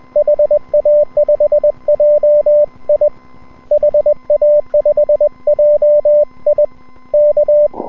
Morse code, heard as a keyed tone, about 21 wpm, 590 Hz. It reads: HA5JI HA5JI K
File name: n6tr / N6TR-7deHA5JIon160M-06-Sep- 31UTC